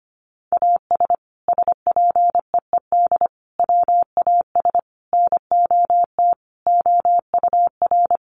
Morse code, heard as a keyed tone, about 25 wpm, 700 Hz. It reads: AS SPEED WAS NOT OUR
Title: Sentences from Top 800 Words 25wpm